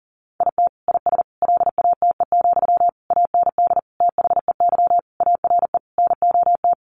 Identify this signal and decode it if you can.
Morse code, heard as a keyed tone, about 40 wpm, 700 Hz. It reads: IT IS LATE, AND THEY ARE NOT
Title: Sentences from Top 600 Words 40wpm